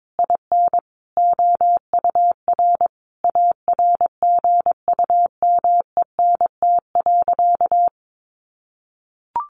Morse code, heard as a keyed tone, about 22 wpm, 700 Hz. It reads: IN OUR ARGUMENT.
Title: Sentences from Top 1500 Words 22wpm